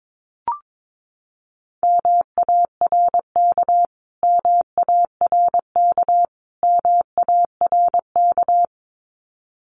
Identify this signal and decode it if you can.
Morse code, heard as a keyed tone, about 22 wpm, 700 Hz. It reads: MARK MARK MARK
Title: Top 500 Words Repeated 22wpm